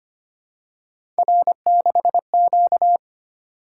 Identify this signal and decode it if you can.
Morse code, heard as a keyed tone, about 25 wpm, 700 Hz. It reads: R6Q
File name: Three Letter-Number Combinations - 25wpm